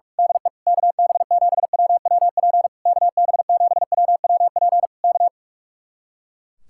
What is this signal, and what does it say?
Morse code, heard as a keyed tone, about 46 wpm, 700 Hz. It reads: DE KB7WWP KB7WWP K
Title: Speed-Racing Common QSO Phrases Encore - 45wpm